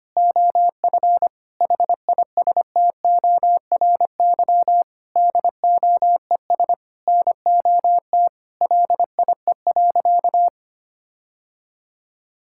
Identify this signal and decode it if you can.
Morse code, heard as a keyed tone, about 25 wpm, 700 Hz. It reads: OF HISTORY DOES NOT LIE.
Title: Sentences from Top 1600 Words 25wpm